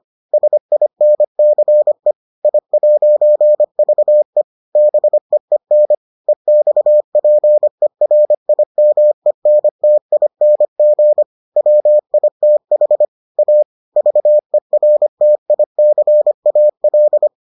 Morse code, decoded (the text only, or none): SINCE I'VE BEEN EXPERIMENTING WITH A VERTICAL